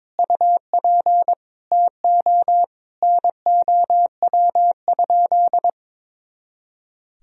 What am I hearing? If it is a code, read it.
Morse code, heard as a keyed tone, about 21 wpm, 700 Hz. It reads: UP TO NOW?
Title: Sentences from Top 100 Words and Abbreviations 22wpm